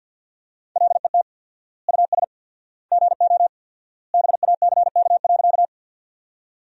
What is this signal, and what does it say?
Morse code, heard as a keyed tone, about 53 wpm, 700 Hz. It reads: LET US GO BACK.
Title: 4x Word-Spacing - Sentences from Top 300 Words - 50wpm